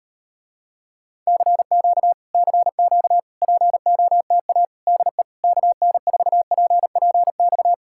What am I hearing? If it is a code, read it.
Morse code, heard as a keyed tone, about 38 wpm, 700 Hz. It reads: CQ CQ POTA DE KN4PPX